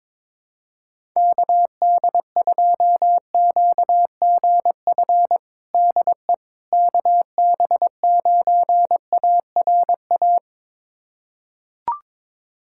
Morse code, heard as a keyed tone, about 22 wpm, 700 Hz. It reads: KD2QGF DE KB9ARA
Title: USA Callsign DE Callsign - 6 Characters - 22wpm